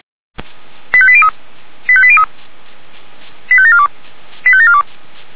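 Five-tone selective call (ZVEI), heard as quick runs of five tones, tones of about 70 ms. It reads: 85803 85803 85742 85742
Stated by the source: ZVEI five-tone selective call received by SDR.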